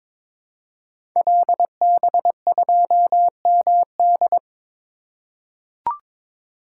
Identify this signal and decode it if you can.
Morse code, heard as a keyed tone, about 22 wpm, 700 Hz. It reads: LB2MD
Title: Code Groups - 5 Letter:Numbers - 22wpm